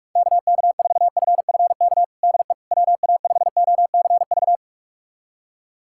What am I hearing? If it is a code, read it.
Morse code, heard as a keyed tone, about 42 wpm, 700 Hz. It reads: KK4FFK DE WA5OCV
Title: USA Callsign DE Callsign - 6 Characters - 45wpm